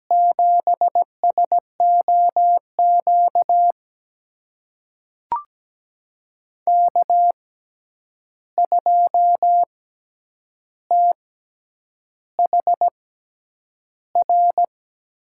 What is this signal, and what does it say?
Morse code, heard as a keyed tone, about 17 wpm, spaced as for 8 wpm, 700 Hz. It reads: 7SOQ K2THR
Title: USA Call Signs - 5 Characters - 6x Character Spacing - 17wpm